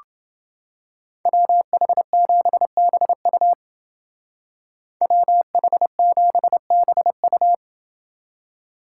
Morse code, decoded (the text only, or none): WH7BU WH7BU